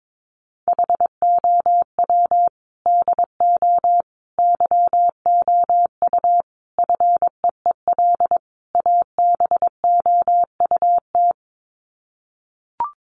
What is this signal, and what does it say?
Morse code, heard as a keyed tone, about 22 wpm, 700 Hz. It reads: HOW DO YOU FEEL ABOUT
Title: Sets of 5 Words 22wpm